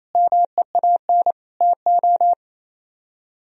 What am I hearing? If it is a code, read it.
Morse code, heard as a keyed tone, about 36 wpm, 700 Hz. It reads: MEAN TO